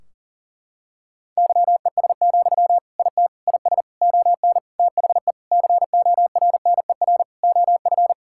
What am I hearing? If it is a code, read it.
Morse code, heard as a keyed tone, about 40 wpm, 700 Hz. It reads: YES, IT IS ON THE CORNER OF
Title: Sentences from Top 1600 Words 40wpm